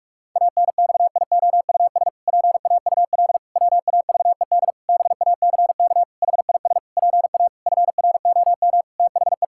Morse code, heard as a keyed tone, about 46 wpm, 700 Hz. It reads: ANXIOUS PAUL WAVED BACK HIS PA FROM THE
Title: Pangram Challenge v2 - 45wpm